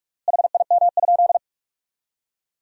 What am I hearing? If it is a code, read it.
Morse code, heard as a keyed tone, about 46 wpm, 700 Hz. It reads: HIM?
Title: 3x Word-Spacing - Sentences from Top 100 Words - 45wpm